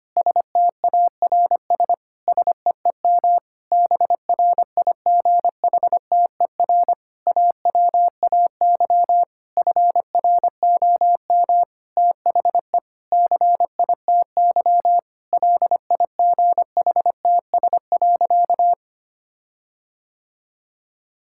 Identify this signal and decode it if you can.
Morse code, heard as a keyed tone, about 25 wpm, 700 Hz. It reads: STARS SEEM BRIGHTER AWAY FROM THE CITY LIGHTS.
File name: practice / 25wpm / Difficult Sentences with Hints - 25wpm